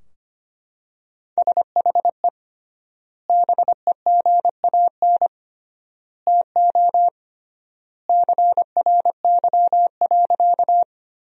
Morse code, heard as a keyed tone, about 25 wpm, 700 Hz. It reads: SHE BEGAN TO CRY.
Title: 3x Word-Spacing - Sentences from Top 500 Words - 25wpm